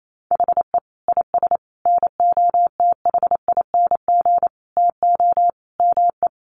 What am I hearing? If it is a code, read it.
Morse code, heard as a keyed tone, about 28 wpm, 700 Hz. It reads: HE IS NOTHING TO ME
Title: Sentences from Top 1200 Words 28wpm